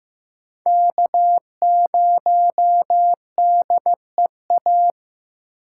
Morse code, heard as a keyed tone, about 15 wpm, 700 Hz. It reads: K0DEA
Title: USA Call Signs - 6x Character Spacing - 15wpm